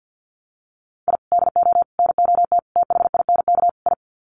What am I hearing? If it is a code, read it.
Morse code, heard as a keyed tone, about 50 wpm, 700 Hz. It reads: I DO NOT THINK I
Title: Sets of 5 Words 50wpm